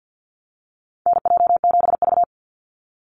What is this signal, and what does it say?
Morse code, heard as a keyed tone, about 50 wpm, 700 Hz. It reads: NJ7V
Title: NA SOTA Activator Callsigns 50wpm